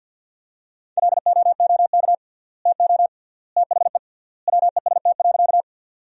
Morse code, heard as a keyed tone, about 50 wpm, 700 Hz. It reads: LOOK TO THE WEST.
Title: 3x Word-Spacing - Sentences from Top 400 Words - 50wpm